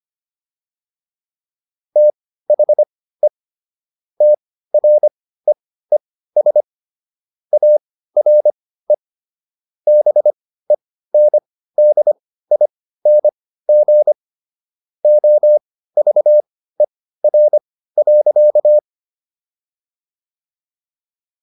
Morse code, heard as a keyed tone, about 25 wpm, spaced as for 15 wpm, 600 Hz. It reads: THE TREES ARE BENDING OVER.